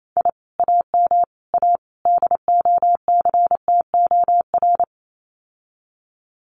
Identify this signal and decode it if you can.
Morse code, heard as a keyed tone, about 28 wpm, 700 Hz. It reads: I AM A DOCTOR
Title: Common QSO Phrases Encore 28wpm